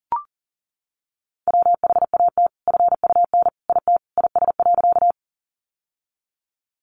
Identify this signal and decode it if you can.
Morse code, heard as a keyed tone, about 40 wpm, 700 Hz. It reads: WHAT FUN IT IS.
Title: Sentences from Top 700 Words 40wpm